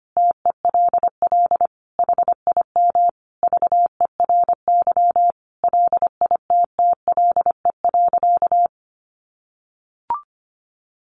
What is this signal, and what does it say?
Morse code, heard as a keyed tone, about 25 wpm, 700 Hz. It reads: TELL HIM VERY LITTLE.
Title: Sentences from Top 800 Words 25wpm